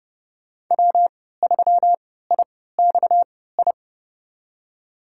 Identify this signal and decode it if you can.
Morse code, heard as a keyed tone, about 30 wpm, spaced as for 17 wpm, 700 Hz. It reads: W3IXI
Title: USA Call Signs - 5 Characters - 3x Character Spacing - 30wpm